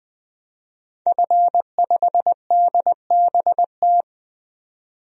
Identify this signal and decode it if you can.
Morse code, heard as a keyed tone, about 20 wpm, 700 Hz. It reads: F5DBT
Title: DXCC-100 Call Signs - 5 Characters - 20wpm